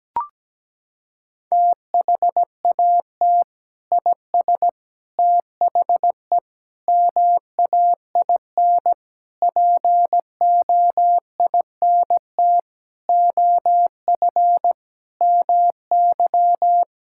Morse code, heard as a keyed tone, about 17 wpm, 700 Hz. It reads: THAT IS THE MAIN POINT OF MY
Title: Sentences from Top 1000 Words - Review Part 5 - 17wpm